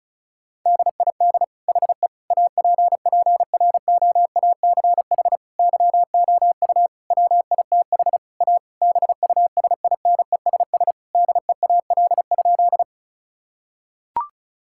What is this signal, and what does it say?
Morse code, heard as a keyed tone, about 36 wpm, 700 Hz. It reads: DID HE APPROACH YOU WITH A BUSINESS DEAL?
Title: Sentences from Top 1000 Words - Review Part 6 - 35wpm